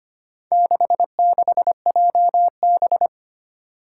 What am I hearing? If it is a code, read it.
Morse code, heard as a keyed tone, about 25 wpm, 700 Hz. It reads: 66JB
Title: Four Challenging Letter-Number Combinations 25wpm